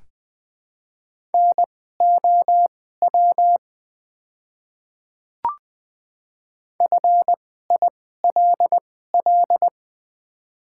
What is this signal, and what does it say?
Morse code, heard as a keyed tone, about 20 wpm, spaced as for 15 wpm, 700 Hz. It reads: NOW FILL
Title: 2x Character Spacing - Top 500 Words - 20wpm